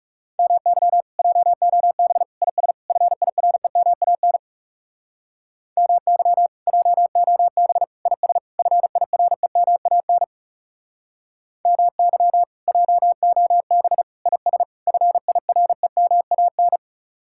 Morse code, heard as a keyed tone, about 36 wpm, 700 Hz. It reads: MY JOB IS FIREMAN MY JOB IS FIREMAN MY JOB IS FIREMAN
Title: Speed-Racing Common QSO Phrases - 30wpm